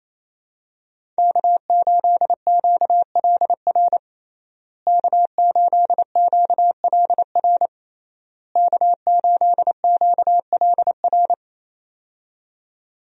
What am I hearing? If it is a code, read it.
Morse code, heard as a keyed tone, about 27 wpm, 700 Hz. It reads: K8QLR K8QLR K8QLR
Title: USA Call Signs - 5 Characters - Repeated 3x - 28wpm